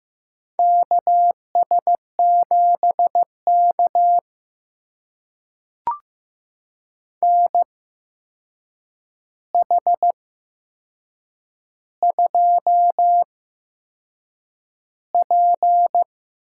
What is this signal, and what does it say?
Morse code, heard as a keyed tone, about 15 wpm, spaced as for 6 wpm, 700 Hz. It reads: KS7K NH2P
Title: USA Call Signs - 8x Character Spacing - 15wpm